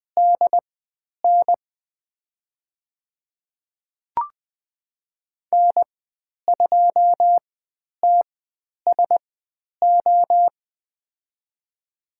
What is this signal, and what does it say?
Morse code, heard as a keyed tone, about 20 wpm, spaced as for 10 wpm, 700 Hz. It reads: DN N2TSO
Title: 095 Lesson 16g - Letter c - Call Signs 20wpm